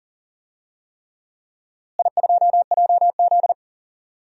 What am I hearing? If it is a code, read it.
Morse code, heard as a keyed tone, about 40 wpm, 700 Hz. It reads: I2JZ
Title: DXCC-100 Call Signs - 4 Characters - Repeated 3x - 40wpm